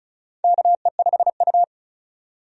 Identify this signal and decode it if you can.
Morse code, heard as a keyed tone, about 34 wpm, 700 Hz. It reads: KE5U
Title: USA Call Signs - 4 Characters - 35wpm